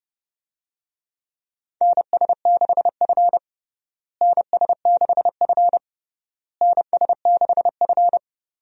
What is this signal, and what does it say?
Morse code, heard as a keyed tone, about 32 wpm, 700 Hz. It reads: NS6F NS6F NS6F
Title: USA Call Signs - 4 Characters - Repeated 3x - 30wpm